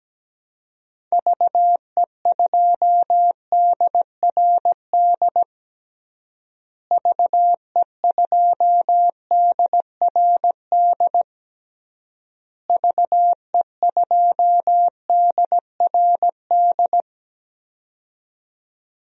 Morse code, heard as a keyed tone, about 17 wpm, 700 Hz. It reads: VE2DRD VE2DRD VE2DRD
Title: DXCC-100 Call Signs - 6 Characters - Repeated 3x - 17wpm